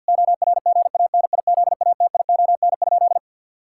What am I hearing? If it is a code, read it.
Morse code, heard as a keyed tone, about 51 wpm, 700 Hz. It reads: ORGANIZATION?
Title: Sentences from Top 1000 Words - Review Part 5 - 50wpm